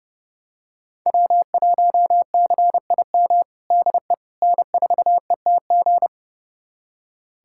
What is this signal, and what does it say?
Morse code, heard as a keyed tone, about 32 wpm, 700 Hz. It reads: W1CIM DE N4ETG
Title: USA Callsign DE Callsign - 5 Characters - 30wpm